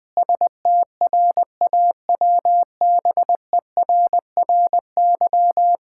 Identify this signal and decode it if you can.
Morse code, heard as a keyed tone, about 21 wpm, 700 Hz. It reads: STRAWBERRY
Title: Pangram Challenge v2 - 20wpm